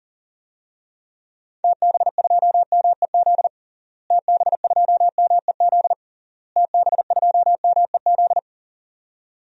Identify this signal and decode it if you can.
Morse code, heard as a keyed tone, about 37 wpm, 700 Hz. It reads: TB2MEZ TB2MEZ TB2MEZ
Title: DXCC-100 Call Signs - 6 Characters - Repeated 3x - 40wpm